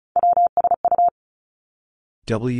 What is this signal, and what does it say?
Morse code, heard as a keyed tone, about 36 wpm, 700 Hz. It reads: WSU
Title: Three Challenging Letter-Number Combinations 35wpm